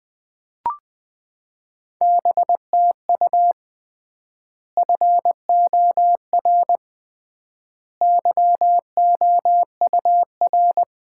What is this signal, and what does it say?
Morse code, heard as a keyed tone, about 21 wpm, 700 Hz. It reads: BTU FOR YOUR
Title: Common QSO Phrases - 3x Word-Spacing - 20wpm